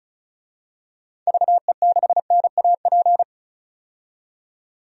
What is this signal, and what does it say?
Morse code, heard as a keyed tone, about 36 wpm, 700 Hz. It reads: VE6NAP
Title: DXCC-100 Call Signs - 6 Characters - 35wpm